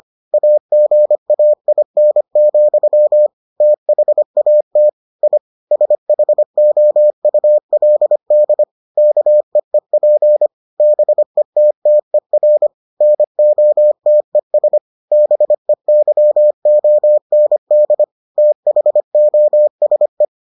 Morse code, heard as a keyed tone, about 25 wpm, 600 Hz. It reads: AGAIN, THAT I SHOULD KEEP BETTER NOTES BEYOND THOSE